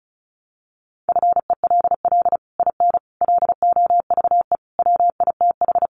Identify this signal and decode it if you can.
Morse code, heard as a keyed tone, about 34 wpm, 700 Hz. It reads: FELL IN LOVE WITH
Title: Sets of 5 Words Encore2 35wpm